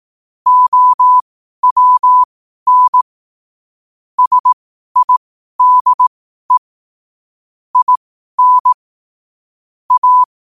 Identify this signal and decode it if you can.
Morse code, heard as a keyed tone, about 18 wpm, spaced as for 12 wpm, 1000 Hz. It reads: OWN SIDE IN A
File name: qotdfiles / 12wpm